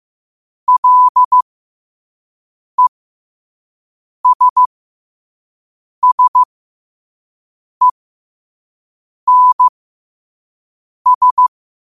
Morse code, heard as a keyed tone, about 15 wpm, spaced as for 5 wpm, 1000 Hz. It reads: LESSENS
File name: qotdfiles / 5wpm